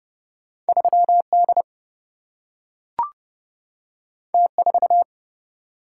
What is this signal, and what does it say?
Morse code, heard as a keyed tone, about 31 wpm, 700 Hz. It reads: 3 D T 4
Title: Single-Letter Single-Number 30wpm